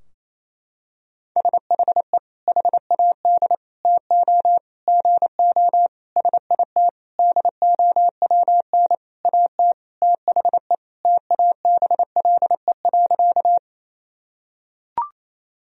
Morse code, heard as a keyed tone, about 28 wpm, 700 Hz. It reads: SHE HAD TO GO SIT DOWN AT THE TABLE.
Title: Sentences from Top 1000 Words - Review Part 2 - 28wpm